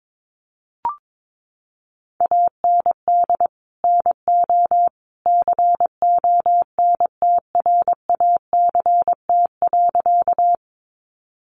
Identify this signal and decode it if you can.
Morse code, heard as a keyed tone, about 22 wpm, 700 Hz. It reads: AND NO CONTRACT.
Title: Sentences from Top 1400 Words 22wpm